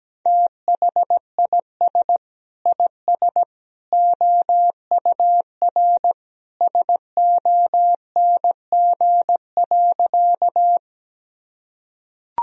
Morse code, heard as a keyed tone, about 17 wpm, 700 Hz. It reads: THIS IS OUR SONG.
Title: Sentences from Top 1000 Words - Review Part 2 - 17wpm